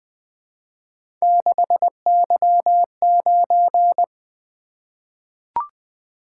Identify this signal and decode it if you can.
Morse code, heard as a keyed tone, about 20 wpm, 700 Hz. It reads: 6Y9
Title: Three Letter-Number Combinations - 20wpm